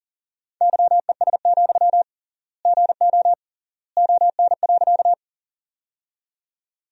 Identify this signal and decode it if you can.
Morse code, heard as a keyed tone, about 40 wpm, 700 Hz. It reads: YES, GO ON.
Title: 3x Word-Spacing - Sentences from Top 500 Words - 40wpm